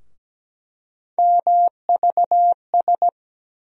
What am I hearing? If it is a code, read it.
Morse code, heard as a keyed tone, about 17 wpm, 700 Hz. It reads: MVS
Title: USA Call Sign Postfixes - 3 Letters 17wpm